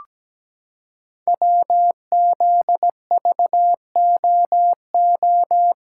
Morse code, heard as a keyed tone, about 17 wpm, 700 Hz. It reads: WZVOO
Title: Code Groups - 5 Letter:Numbers - 17wpm